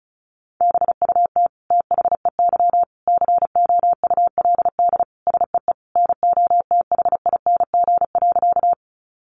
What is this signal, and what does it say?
Morse code, heard as a keyed tone, about 34 wpm, 700 Hz. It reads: BUT THEY COULD SEE NOTHING.
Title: Sentences from Top 500 Words 35wpm